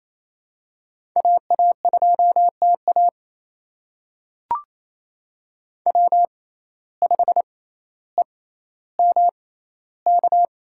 Morse code, heard as a keyed tone, about 28 wpm, spaced as for 14 wpm, 700 Hz. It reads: AA2TA W5EMK
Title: USA Call Signs - 5 Characters - 6x Character Spacing - 28wpm